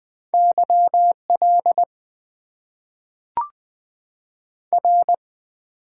Morse code, heard as a keyed tone, about 20 wpm, 700 Hz. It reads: YL R
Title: Top 90 QSO Elements - 8x Character Spacing - 20wpm